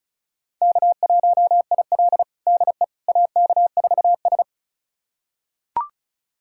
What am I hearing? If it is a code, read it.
Morse code, heard as a keyed tone, about 36 wpm, 700 Hz. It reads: K1IL DE AK4S
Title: USA Callsign DE Callsign - 4 Characters - 35wpm